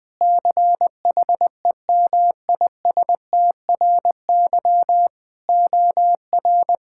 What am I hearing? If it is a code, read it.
Morse code, heard as a keyed tone, about 20 wpm, 700 Hz. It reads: CHEMISTRY OR
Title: Sentences from Top 1000 Words - Review Part 6 - 20wpm